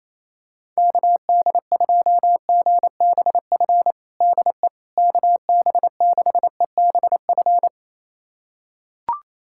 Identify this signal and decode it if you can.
Morse code, heard as a keyed tone, about 28 wpm, 700 Hz. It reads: KD2GBF DE KB6EBF
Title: USA Callsign DE Callsign 28wpm